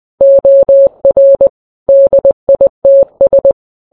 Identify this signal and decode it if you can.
Morse code, heard as a keyed tone, about 20 wpm, 550 Hz. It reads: OR DITS